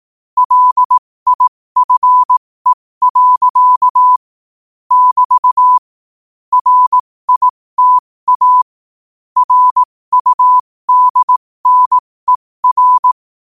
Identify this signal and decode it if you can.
Morse code, heard as a keyed tone, about 18 wpm, spaced as for 15 wpm, 1000 Hz. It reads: LIFE. <BT> RITA RUDNER
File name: qotdfiles / 15wpm